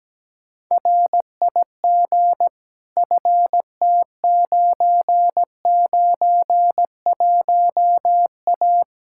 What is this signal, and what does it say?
Morse code, heard as a keyed tone, about 17 wpm, 700 Hz. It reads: RIG FT991A
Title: K1USN SST QSO Practice 17wpm